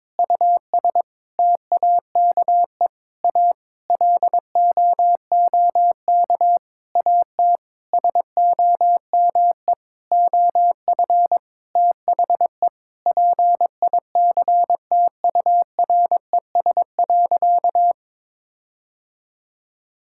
Morse code, heard as a keyed tone, about 22 wpm, 700 Hz. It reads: US TAKE A LOOK AT SOME OF THE PICTURES.
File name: practice / 22wpm / Sentences with 8-Word Phrases - 22wpm